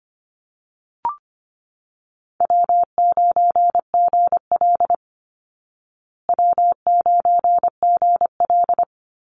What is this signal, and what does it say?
Morse code, heard as a keyed tone, about 25 wpm, 700 Hz. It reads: W9GL W9GL
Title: Call Signs Repeated 3x - 25wpm